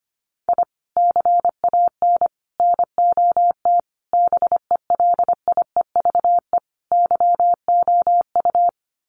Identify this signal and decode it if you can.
Morse code, heard as a keyed tone, about 25 wpm, 700 Hz. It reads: I CAN NOT BELIEVE YOU
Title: Sets of 5 Words Encore 25wpm